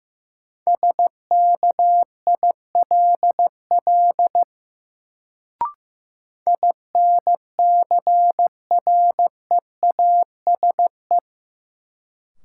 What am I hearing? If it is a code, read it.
Morse code, heard as a keyed tone, about 15 wpm, 700 Hz. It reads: SKILL INCREASE
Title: Sentences from Top 1200 Words - New Word Drill Alt - 15wpm